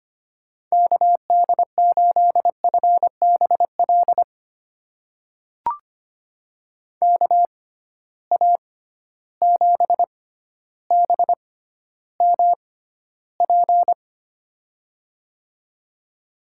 Morse code, heard as a keyed tone, about 25 wpm, spaced as for 13 wpm, 700 Hz. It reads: KD8FBL KA7BMP
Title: USA Call Signs - 6 Characters - 6x Character Spacing - 25wpm